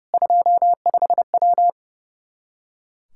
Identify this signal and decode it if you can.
Morse code, heard as a keyed tone, about 30 wpm, 700 Hz. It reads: 25W
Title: K1USN SST QSO Practice 30wpm